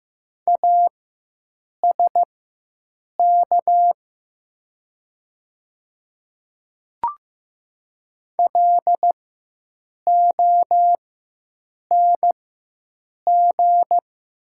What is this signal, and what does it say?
Morse code, heard as a keyed tone, about 15 wpm, spaced as for 7 wpm, 700 Hz. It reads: ASK LONG
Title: 4x Character Spacing - Top 300 Words - 15wpm